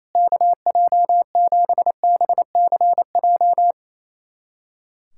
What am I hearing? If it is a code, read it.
Morse code, heard as a keyed tone, about 28 wpm, 700 Hz. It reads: KJ7BCJ
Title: USA Callsign DE Callsign 28wpm